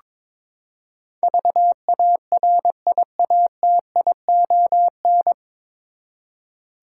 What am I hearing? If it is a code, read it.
Morse code, heard as a keyed tone, about 22 wpm, 700 Hz. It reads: VARIATION
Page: Morse Code Ninja practice 22wpm